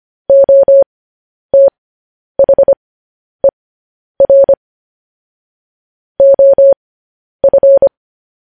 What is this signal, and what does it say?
Morse code, heard as a keyed tone, about 25 wpm, spaced as for 10 wpm, 550 Hz. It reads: OTHER OF
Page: News Headlines in Morse Code at 10 WPM